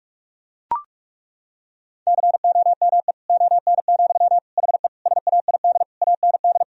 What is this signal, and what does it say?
Morse code, heard as a keyed tone, about 41 wpm, 700 Hz. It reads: COME ON, HE SAID AND
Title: Sentences from Top 1000 Words - Review Part 2 - 45wpm